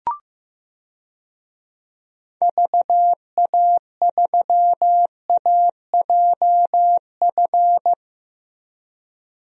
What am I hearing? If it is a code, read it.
Morse code, heard as a keyed tone, about 15 wpm, 700 Hz. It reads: VA3AJF
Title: DXCC-100 Call Signs - 6 Characters - Repeated 3x - 15wpm